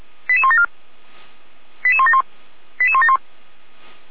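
Five-tone selective call (ZVEI), heard as quick runs of five tones, tones of about 70 ms. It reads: 88174 88171 88171
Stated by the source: ZVEI five-tone selective call received on an SDR, fire brigade feuerwehr-orlamuende.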